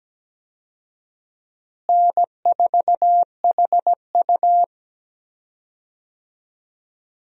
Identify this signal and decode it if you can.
Morse code, heard as a keyed tone, about 17 wpm, 700 Hz. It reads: N4HU
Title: USA Call Signs - 4 Characters - Repeated 3x - 17wpm